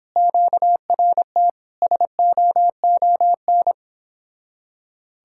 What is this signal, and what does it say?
Morse code, heard as a keyed tone, about 31 wpm, 700 Hz. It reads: QRT SOON